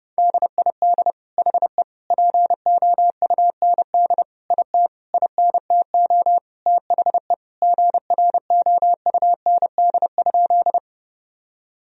Morse code, heard as a keyed tone, about 32 wpm, 700 Hz. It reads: DID HE POUND IT INTO THE GROUND?
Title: Sentences from Top 1000 Words - Review Part 2 - 30wpm